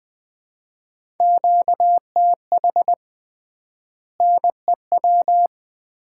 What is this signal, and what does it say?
Morse code, heard as a keyed tone, about 20 wpm, 700 Hz. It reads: QTH NEW
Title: Common QSO Phrases - 3x Word-Spacing - 20wpm